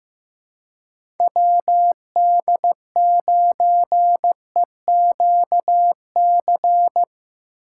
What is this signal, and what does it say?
Morse code, heard as a keyed tone, about 15 wpm, 700 Hz. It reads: WD9EQC
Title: USA Call Signs - 6 Characters - 2x Character Spacing - 15wpm